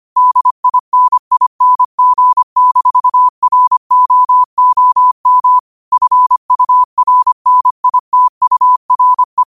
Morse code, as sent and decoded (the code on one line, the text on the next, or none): -.. .. -. .. -. --. -....- .-. --- --- -- / ..-. ..- .-. -. .. - ..- .-. .
DINING-ROOM FURNITURE